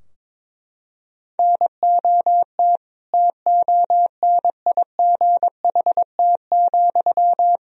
Morse code, heard as a keyed tone, about 22 wpm, 700 Hz. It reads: NOT TONIGHT,
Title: Sentences from Top 1000 Words - Review Part 3 - 22wpm